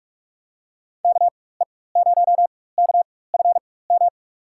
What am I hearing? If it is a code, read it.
Morse code, heard as a keyed tone, about 46 wpm, spaced as for 21 wpm, 700 Hz. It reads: KE0KFM
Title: USA Call Signs - 4x Character Spacing - 45wpm